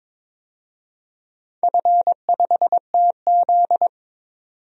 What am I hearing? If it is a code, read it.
Morse code, heard as a keyed tone, about 22 wpm, 700 Hz. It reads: F5TZ
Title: DXCC-100 Call Signs - 4 Characters - 22wpm